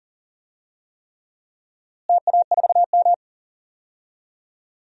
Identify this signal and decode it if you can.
Morse code, heard as a keyed tone, about 40 wpm, 700 Hz. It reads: TA4M
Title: DXCC-100 Call Signs - 4 Characters - Repeated 3x - 40wpm